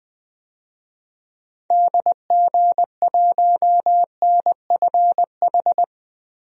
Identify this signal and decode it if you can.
Morse code, heard as a keyed tone, about 20 wpm, 700 Hz. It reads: DG1NFH
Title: DXCC-100 Call Signs - 6 Characters - Repeated 3x - 20wpm